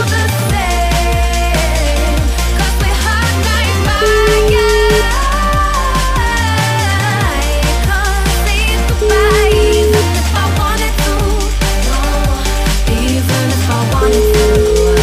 Ringback tone (Europe, 425 Hz), with noise.